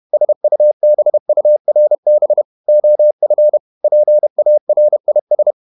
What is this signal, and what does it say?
Morse code, heard as a keyed tone, about 31 wpm, 600 Hz. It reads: SUBURB OF PARIS
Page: Never 31wpm | CW med Gnister